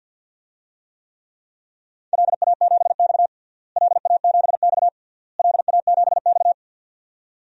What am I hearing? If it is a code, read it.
Morse code, heard as a keyed tone, about 50 wpm, 700 Hz. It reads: LA7X LA7X LA7X
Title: DXCC-100 Call Signs - 4 Characters - Repeated 3x - 50wpm